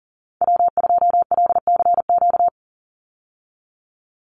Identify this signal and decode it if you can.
Morse code, heard as a keyed tone, about 40 wpm, 700 Hz. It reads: W2LCQ
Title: Call Signs Repeated 2x - 40wpm